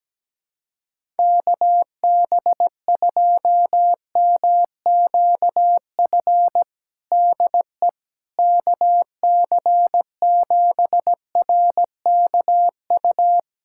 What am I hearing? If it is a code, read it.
Morse code, heard as a keyed tone, about 17 wpm, 700 Hz. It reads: KB2MQF DE KC7RKU
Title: USA Callsign DE Callsign 17wpm